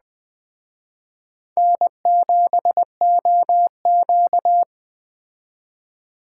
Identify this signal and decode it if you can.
Morse code, heard as a keyed tone, about 20 wpm, 700 Hz. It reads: N7OQ
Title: Speed Racing - USA 1x2 Call Signs 17wpm